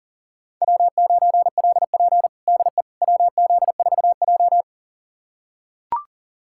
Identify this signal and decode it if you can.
Morse code, heard as a keyed tone, about 40 wpm, 690 Hz. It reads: W9LP DE WZ4J
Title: USA Callsign DE Callsign - 4 Characters - 40wpm